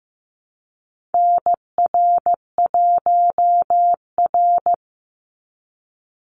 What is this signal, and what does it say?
Morse code, heard as a keyed tone, about 14 wpm, 700 Hz. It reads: NR1R
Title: Call Signs Repeated 3x - 15wpm